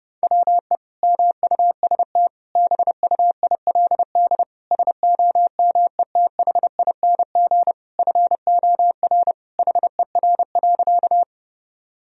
Morse code, heard as a keyed tone, about 32 wpm, 700 Hz. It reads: WE MUST BUILD SOMETHING FOR HER.
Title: Sentences from Top 1000 Words - Review Part 1 - 30wpm